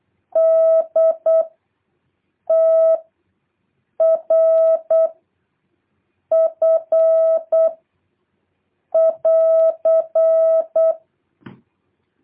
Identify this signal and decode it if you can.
Morse code, heard as a keyed tone, about 8 wpm, 650 Hz. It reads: D T R F <AR>
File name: jh8cbh / 2023cw